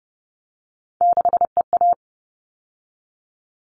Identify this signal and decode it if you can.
Morse code, heard as a keyed tone, about 30 wpm, 700 Hz. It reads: 6EA
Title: All Cut Numbers - 3-Digit 30wpm